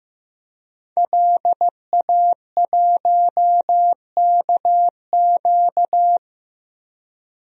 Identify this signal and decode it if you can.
Morse code, heard as a keyed tone, about 15 wpm, 700 Hz. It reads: LA1KQ
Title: DXCC-100 Call Signs - 5 Characters - Repeated 3x - 15wpm